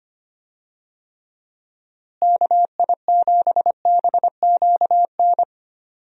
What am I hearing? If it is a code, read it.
Morse code, heard as a keyed tone, about 25 wpm, 700 Hz. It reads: KI7BQN
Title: USA Call Signs - 6 Characters - Repeated 3x - 25wpm